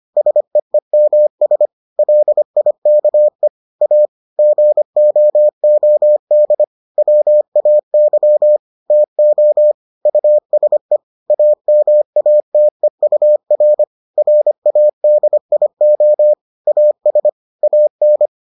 Morse code, heard as a keyed tone, about 25 wpm, 600 Hz. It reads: SEEMS LIKE A GOOD WAY TO USE AMATEUR RADIO AS AN